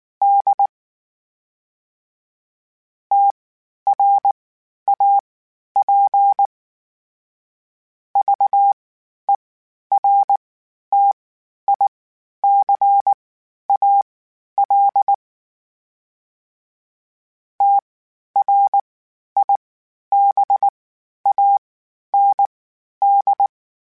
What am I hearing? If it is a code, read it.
Morse code, heard as a keyed tone, about 19 wpm, spaced as for 11 wpm, 800 Hz. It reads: D TRAP VERTICAL TRIBAND
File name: (spaced) / FromSMs spaced mp3s / Antennas sp@18